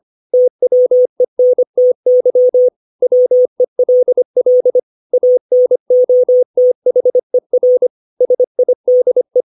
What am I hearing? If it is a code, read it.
Morse code, heard as a keyed tone, about 25 wpm, 500 Hz. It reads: TWENTY WELL ANOTHER SIDE
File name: 632 words / 632 Words 25WPM